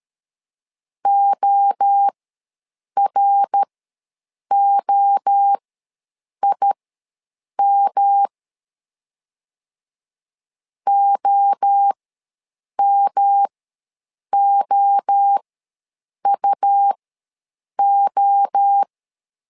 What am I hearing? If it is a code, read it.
Morse code, heard as a keyed tone, about 13 wpm, spaced as for 7 wpm, 800 Hz. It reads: OROIM OMOUO
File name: LEZIONI / 3 MWRO / 3 MWRO ws 12wpm 1914